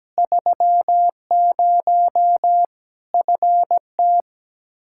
Morse code, heard as a keyed tone, about 17 wpm, 700 Hz. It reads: 30 FT
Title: Practice QSO - Exchange 2 - Info - 17wpm